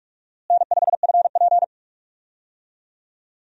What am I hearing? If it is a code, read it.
Morse code, heard as a keyed tone, about 42 wpm, 700 Hz. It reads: N5FP
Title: USA Call Signs - 4 Characters - Repeated 2x - 45wpm